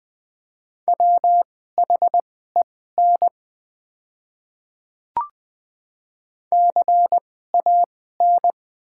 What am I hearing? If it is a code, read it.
Morse code, heard as a keyed tone, about 20 wpm, spaced as for 14 wpm, 690 Hz. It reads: WHEN CAN
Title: 2x Character Spacing - Top 100 Words - 20wpm